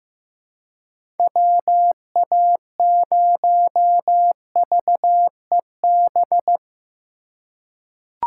Morse code, heard as a keyed tone, about 15 wpm, 700 Hz. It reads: WA0VEB
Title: USA Call Signs - 3x Character Spacing - 15wpm